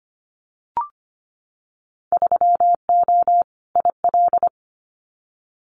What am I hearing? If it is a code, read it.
Morse code, heard as a keyed tone, about 25 wpm, 690 Hz. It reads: 3O IL
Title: Winter Field Day Exchanges 25wpm